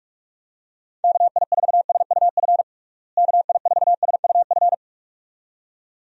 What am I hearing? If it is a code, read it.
Morse code, heard as a keyed tone, about 46 wpm, 700 Hz. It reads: KI4SUF KI4SUF
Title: USA Call Signs - 6 Characters - Repeated 2x - 45wpm